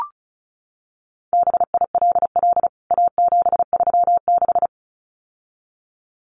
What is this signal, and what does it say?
Morse code, heard as a keyed tone, about 34 wpm, 700 Hz. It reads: BILL A736
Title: CWT Exchange 35wpm